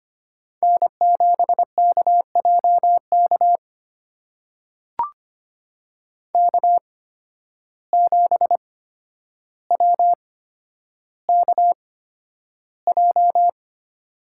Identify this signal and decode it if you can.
Morse code, heard as a keyed tone, about 25 wpm, spaced as for 11 wpm, 700 Hz. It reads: N7KJK K7WKJ
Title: USA Call Signs - 5 Characters - 8x Character Spacing - 25wpm